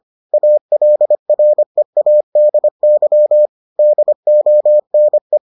Morse code, heard as a keyed tone, about 25 wpm, 600 Hz. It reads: ALREADY DONE